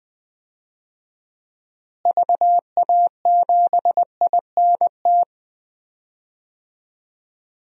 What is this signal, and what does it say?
Morse code, heard as a keyed tone, about 20 wpm, 700 Hz. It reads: VA7INT
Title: DXCC-100 Call Signs - 6 Characters - Repeated 3x - 20wpm